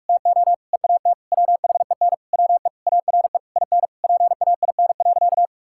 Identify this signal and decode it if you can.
Morse code, heard as a keyed tone, about 46 wpm, 700 Hz. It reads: TO EAT WHEN WE ARE IN PAIN.
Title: Sentences from Top 1600 Words 45wpm